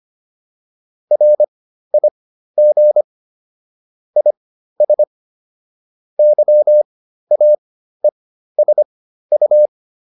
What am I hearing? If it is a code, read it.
Morse code, heard as a keyed tone, about 25 wpm, spaced as for 13 wpm, 600 Hz. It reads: RIG IS YAESU
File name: QSO401 13